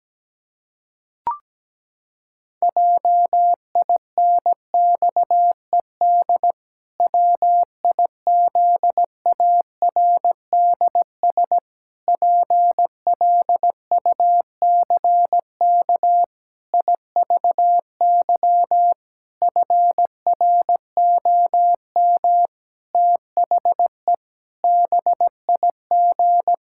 Morse code, heard as a keyed tone, about 17 wpm, 700 Hz. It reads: JINXED WIZARDS PLUCK IVY FROM THE BIG
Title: Pangram Challenge v2 - 17wpm